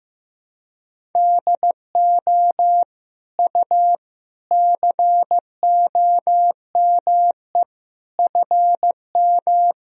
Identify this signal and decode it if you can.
Morse code, heard as a keyed tone, about 16 wpm, 700 Hz. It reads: DO U COME FM
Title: Sentences from Top 200 Words and Abbreviations 15wpm